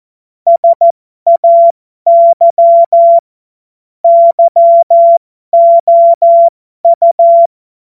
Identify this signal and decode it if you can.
Morse code, heard as a keyed tone, about 14 wpm, spaced as for 12 wpm, 680 Hz. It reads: SAY YOU